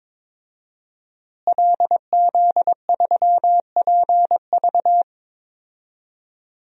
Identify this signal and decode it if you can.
Morse code, heard as a keyed tone, about 22 wpm, 700 Hz. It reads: LZ3PV
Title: DXCC-100 Call Signs - 5 Characters - Repeated 2x - 22wpm